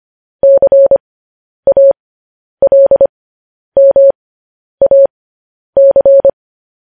Morse code, heard as a keyed tone, about 25 wpm, spaced as for 10 wpm, 550 Hz. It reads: CALMAC
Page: news-headlines-in-morse-code-at-10-wpm